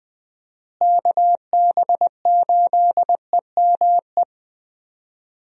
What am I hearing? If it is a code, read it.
Morse code, heard as a keyed tone, about 21 wpm, 700 Hz. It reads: KB8EME
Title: USA Call Signs - 6 Characters - Repeated 3x - 20wpm